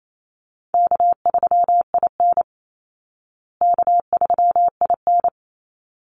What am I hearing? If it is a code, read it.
Morse code, heard as a keyed tone, about 28 wpm, 700 Hz. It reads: K3IN K3IN
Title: Easy Call Signs Repeated 2x - 28wpm